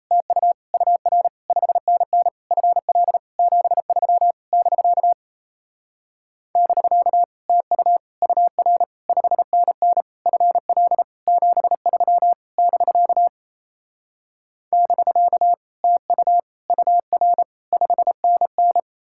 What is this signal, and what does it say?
Morse code, heard as a keyed tone, about 38 wpm, 700 Hz. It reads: TU UR 5NN FL 73 <BK> <BK> TU UR 5NN FL 73 <BK> <BK> TU UR 5NN